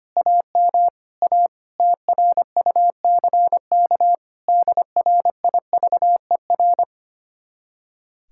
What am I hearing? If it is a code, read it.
Morse code, heard as a keyed tone, about 32 wpm, 700 Hz. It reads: AM A TRUCK DRIVER